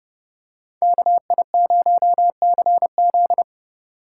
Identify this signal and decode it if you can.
Morse code, heard as a keyed tone, about 29 wpm, 700 Hz. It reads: KI0CZ